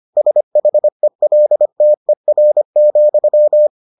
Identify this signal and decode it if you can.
Morse code, heard as a keyed tone, about 25 wpm, 600 Hz. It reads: SHELTER,